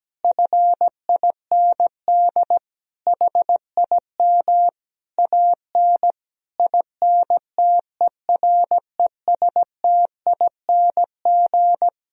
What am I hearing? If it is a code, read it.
Morse code, heard as a keyed tone, about 17 wpm, 700 Hz. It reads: FIND HIM AN INTERESTING
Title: Sentences from Top 1000 Words - Review Part 1 - 17wpm